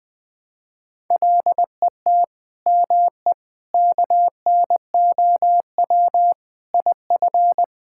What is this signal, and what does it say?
Morse code, heard as a keyed tone, about 21 wpm, 700 Hz. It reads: LET ME KNOW IF
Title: Sets of 8 Words - 20wpm